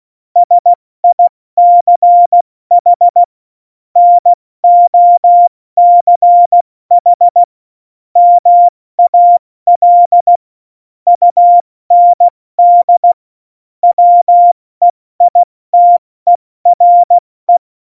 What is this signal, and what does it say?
Morse code, heard as a keyed tone, about 16 wpm, 700 Hz. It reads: SICH NOCH MAL UND WEITERE
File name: Morsekurs / Witze / Witz 03.Juli 201, 70 F80